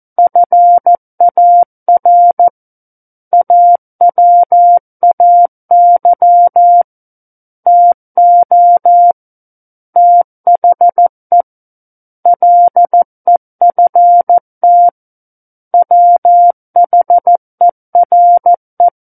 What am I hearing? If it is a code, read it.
Morse code, heard as a keyed tone, about 14 wpm, 700 Hz. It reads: FAR AWAY TO THE LEFT WHERE
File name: Book One / 05-Chapter 5 - 14 WPM